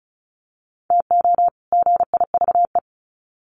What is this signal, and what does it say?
Morse code, heard as a keyed tone, about 34 wpm, 700 Hz. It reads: TO GIVE
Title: Sets of 2 Words 35wpm